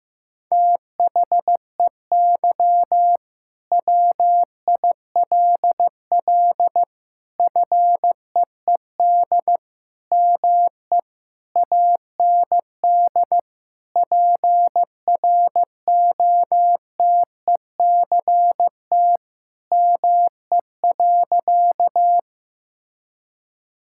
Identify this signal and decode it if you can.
Morse code, heard as a keyed tone, about 15 wpm, 700 Hz. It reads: THEY WILL FEED ME AND PROTECT ME.
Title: Sentences from Top 1600 Words 15wpm